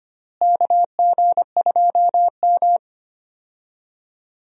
Morse code, heard as a keyed tone, about 25 wpm, 700 Hz. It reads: KG2M
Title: USA Call Signs - 4 Characters - Repeated 2x - 25wpm